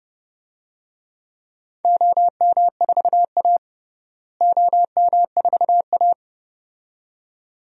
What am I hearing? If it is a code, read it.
Morse code, heard as a keyed tone, about 32 wpm, 700 Hz. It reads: OM4A OM4A
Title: DXCC-100 Call Signs - 4 Characters - Repeated 2x - 30wpm